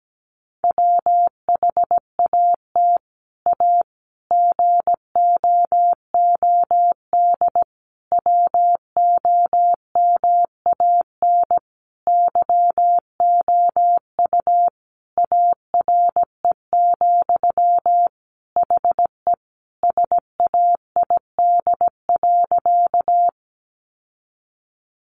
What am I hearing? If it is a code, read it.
Morse code, heard as a keyed tone, about 17 wpm, 700 Hz. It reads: WHAT A GOOD WOMAN YOU ARE, HE SAID.
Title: Sentences from Top 1100 Words 17wpm